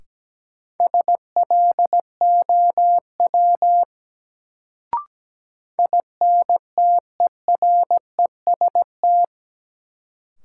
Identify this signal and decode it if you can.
Morse code, heard as a keyed tone, about 17 wpm, 700 Hz. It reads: SLOW INTEREST
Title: Sentences from Top 400 Words - New Word Drill - 17wpm